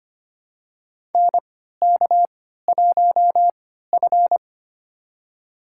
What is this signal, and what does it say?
Morse code, heard as a keyed tone, about 25 wpm, spaced as for 14 wpm, 700 Hz. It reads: NK1F
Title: USA Call Signs - 3x Character Spacing - 25wpm